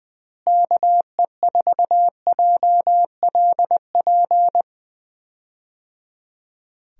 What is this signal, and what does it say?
Morse code, heard as a keyed tone, about 21 wpm, 700 Hz. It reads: KE4JLP
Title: USA Call Signs - 6 Characters - Repeated 3x - 20wpm